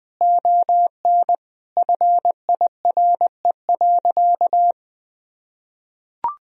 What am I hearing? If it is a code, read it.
Morse code, heard as a keyed tone, about 20 wpm, 700 Hz. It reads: ON FIRE.
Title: Sentences from Top 1000 Words - Review Part 2 - 20wpm